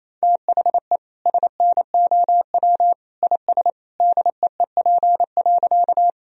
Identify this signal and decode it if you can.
Morse code, heard as a keyed tone, about 28 wpm, 700 Hz. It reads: THE SNOW IS DEEP.
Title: Sentences from Top 1000 Words - Review Part 3 - 28wpm